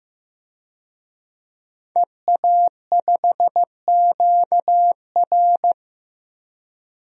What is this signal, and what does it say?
Morse code, heard as a keyed tone, about 15 wpm, 700 Hz. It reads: EA5QR
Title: DXCC-100 Call Signs - 5 Characters - Repeated 2x - 15wpm